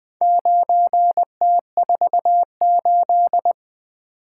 Morse code, heard as a keyed tone, about 20 wpm, 700 Hz. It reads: 9T48
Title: Four Letter-Number Combinations - 20wpm